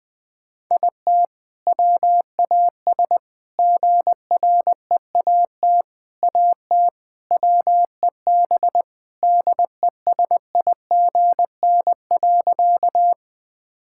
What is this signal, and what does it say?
Morse code, heard as a keyed tone, about 20 wpm, 700 Hz. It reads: IT WAS GREAT AT WEB DESIGN.
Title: All-Age Appropriate Jokes - 20wpm